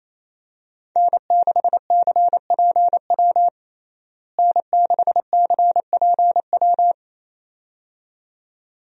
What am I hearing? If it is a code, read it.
Morse code, heard as a keyed tone, about 28 wpm, 700 Hz. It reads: N6CPW N6CPW
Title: USA Call Signs - 5 Characters - Repeated 2x - 28wpm